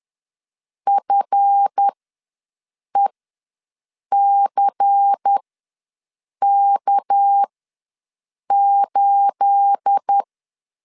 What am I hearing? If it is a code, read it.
Morse code, heard as a keyed tone, about 10 wpm, spaced as for 6 wpm, 800 Hz. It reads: FECK8
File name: LEZIONI / 9 286 ws 10wpm 4032